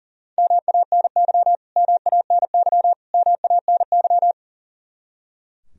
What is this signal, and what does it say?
Morse code, heard as a keyed tone, about 40 wpm, 700 Hz. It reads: MANY MANY MANY
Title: Top 100 Words Repeated 40wpm